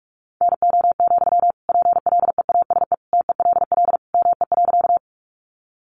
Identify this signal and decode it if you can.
Morse code, heard as a keyed tone, about 44 wpm, 700 Hz. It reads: NO, PLEASE TELL ME.
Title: Sentences from Top 1500 Words 45wpm